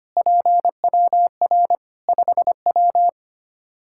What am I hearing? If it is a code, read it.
Morse code, heard as a keyed tone, about 25 wpm, 700 Hz. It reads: PWR 5W
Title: Practice QSO - Exchange 2 - Info - 25wpm